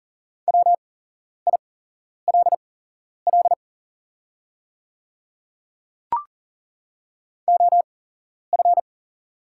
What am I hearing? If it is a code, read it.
Morse code, heard as a keyed tone, about 40 wpm, spaced as for 11 wpm, 700 Hz. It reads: WILL OF
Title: 8x Character Spacing - Top 100 Words - 40wpm